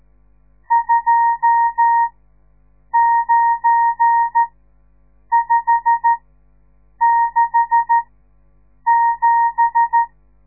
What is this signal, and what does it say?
Morse code, heard as a keyed tone, about 13 wpm, spaced as for 7 wpm, 930 Hz. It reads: 29567
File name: radiomorningstar / morse / les 10